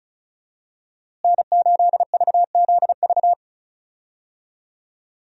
Morse code, heard as a keyed tone, about 36 wpm, 700 Hz. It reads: N8VZV
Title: USA Call Signs - 5 Characters - Repeated 3x - 35wpm